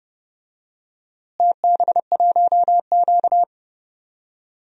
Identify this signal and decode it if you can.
Morse code, heard as a keyed tone, about 31 wpm, 700 Hz. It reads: TB1Q
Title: DXCC-100 Call Signs - 4 Characters - Repeated 3x - 30wpm